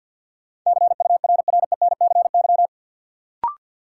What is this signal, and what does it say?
Morse code, heard as a keyed tone, about 51 wpm, 700 Hz. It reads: CURRENCY
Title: Sentences from Top 1500 Words - New Word Drill - 50wpm